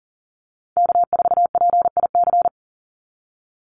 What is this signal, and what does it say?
Morse code, heard as a keyed tone, about 40 wpm, 700 Hz. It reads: K4PIC
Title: NA SOTA Chaser Callsigns 40wpm